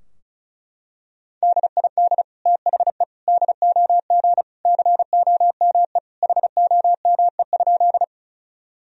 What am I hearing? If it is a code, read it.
Morse code, heard as a keyed tone, about 34 wpm, 700 Hz. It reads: DID THE DOG COME HOME?
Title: Sentences from Top 1000 Words - Review Part 2 - 35wpm